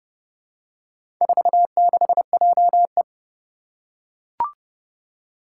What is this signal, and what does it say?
Morse code, heard as a keyed tone, about 30 wpm, 700 Hz. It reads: 46JE
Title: Four Challenging Letter-Number Combinations 30wpm